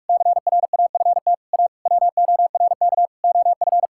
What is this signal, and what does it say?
Morse code, heard as a keyed tone, about 41 wpm, 700 Hz. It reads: CRAFT A WORK OF
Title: Sentences from Top 1000 Words - Review Part 6 - 45wpm